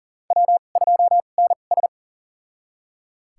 Morse code, heard as a keyed tone, about 40 wpm, spaced as for 29 wpm, 700 Hz. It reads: W2NS
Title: USA Call Signs - 4 Characters - 2x Character Spacing - 40wpm